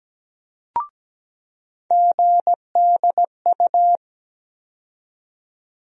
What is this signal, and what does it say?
Morse code, heard as a keyed tone, about 17 wpm, 700 Hz. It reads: GDU
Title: USA Call Sign Postfixes - 3 Letters 17wpm